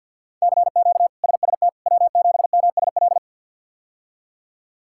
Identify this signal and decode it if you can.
Morse code, heard as a keyed tone, about 49 wpm, 700 Hz. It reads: CQ SST W7MSL
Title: K1USN SST QSO Practice - 50wpm